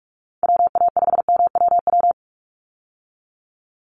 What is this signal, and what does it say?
Morse code, heard as a keyed tone, about 44 wpm, 700 Hz. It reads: WA5MWW
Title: USA Call Signs 45wpm